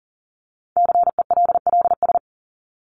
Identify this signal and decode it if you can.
Morse code, heard as a keyed tone, about 40 wpm, 700 Hz. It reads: CELLS
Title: ES:S Suffix - Top 100 Words with Base Words 40wpm